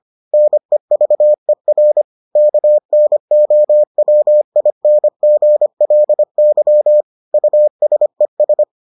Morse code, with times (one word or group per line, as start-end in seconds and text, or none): 0.3-2.0 NEVER
2.4-7.0 KNOWINGLY
7.3-8.6 USES